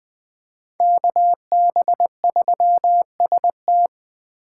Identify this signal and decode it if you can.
Morse code, heard as a keyed tone, about 20 wpm, 700 Hz. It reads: KB3ST
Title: Speed Racing - USA 2x2 Call Signs 17wpm